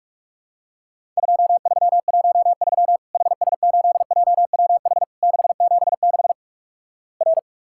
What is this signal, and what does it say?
Morse code, heard as a keyed tone, about 46 wpm, 700 Hz. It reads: 2313 HS8JWH 676 R
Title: Worked All Europe DX Contest - QTC Exchange - 45wpm